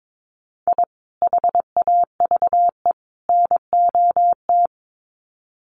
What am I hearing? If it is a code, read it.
Morse code, heard as a keyed tone, about 22 wpm, 700 Hz. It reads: I HAVE NOT
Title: Sets of 3 Words 22wpm